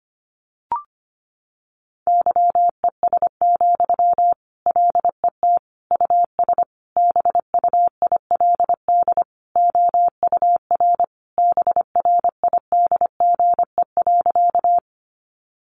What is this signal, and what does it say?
Morse code, heard as a keyed tone, about 25 wpm, 700 Hz. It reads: YES, LET US BUILD OUR BRIDGE.
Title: Sentences from Top 1200 Words 25wpm